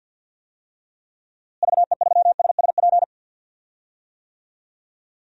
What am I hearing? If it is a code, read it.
Morse code, heard as a keyed tone, about 46 wpm, 700 Hz. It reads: VE3SSP
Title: DXCC-100 Call Signs - 6 Characters - Repeated 3x - 50wpm